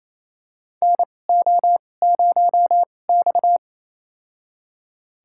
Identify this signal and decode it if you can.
Morse code, heard as a keyed tone, about 28 wpm, spaced as for 20 wpm, 700 Hz. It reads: NO0X
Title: USA Call Signs - 4 Characters - 2x Character Spacing - 28wpm